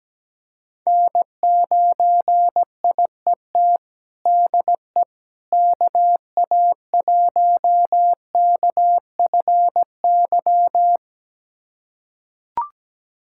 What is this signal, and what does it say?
Morse code, heard as a keyed tone, about 17 wpm, 700 Hz. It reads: N9IET DE KA1KFY
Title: USA Callsign DE Callsign 17wpm